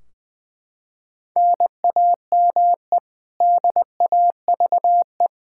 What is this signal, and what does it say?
Morse code, heard as a keyed tone, about 20 wpm, 700 Hz. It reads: NAME DAVE
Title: Practice QSO - Exchange 1 - Beginning - 20wpm